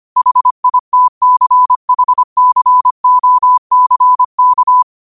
Morse code, heard as a keyed tone, about 25 wpm, 1000 Hz. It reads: HITCHCOCK